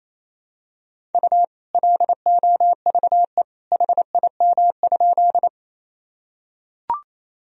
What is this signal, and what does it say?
Morse code, heard as a keyed tone, about 28 wpm, 700 Hz. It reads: U LOVE HIM?
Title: Sentences from Top 400 Words and Abbreviations 28wpm